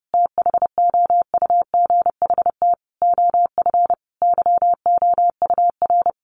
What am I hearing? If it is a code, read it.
Morse code, heard as a keyed tone, about 30 wpm, 700 Hz. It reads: THOUGHT OF YOUR
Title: Sentences from Top 700 Words 30wpm